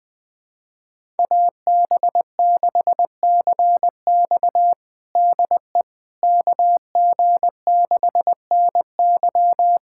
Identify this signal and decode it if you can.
Morse code, heard as a keyed tone, about 20 wpm, 700 Hz. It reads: AB6CX DE KG6NY
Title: USA Callsign DE Callsign - 5 Characters - 20wpm